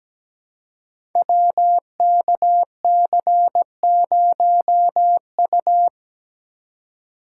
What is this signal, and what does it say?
Morse code, heard as a keyed tone, about 17 wpm, 700 Hz. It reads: WKC0U
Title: Code Groups - 5 Letter:Numbers - 17wpm